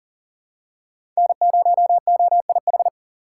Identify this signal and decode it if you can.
Morse code, heard as a keyed tone, about 38 wpm, 700 Hz. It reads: N0OIH
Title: USA Call Signs - 5 Characters - 4x Character Spacing - 40wpm